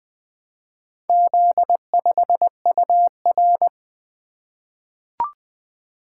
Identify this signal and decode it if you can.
Morse code, heard as a keyed tone, about 21 wpm, 700 Hz. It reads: Z5UR
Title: Four Letter-Number Combinations - 20wpm